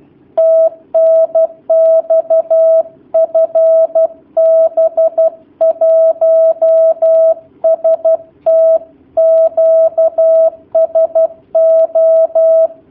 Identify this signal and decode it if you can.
Morse code, heard as a keyed tone, about 12 wpm, 650 Hz. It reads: TNX FB 1ST QSO
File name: jh8cbh / 2019cw / sp11- w